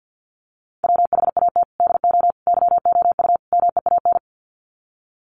Morse code, heard as a keyed tone, about 48 wpm, 700 Hz. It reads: WHAT DO YOU MEAN
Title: Sets of 4 Words 50wpm